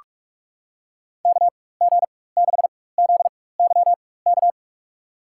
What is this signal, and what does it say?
Morse code, heard as a keyed tone, about 46 wpm, spaced as for 21 wpm, 690 Hz. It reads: KG6ZYK
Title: USA Call Signs - 4x Character Spacing - 45wpm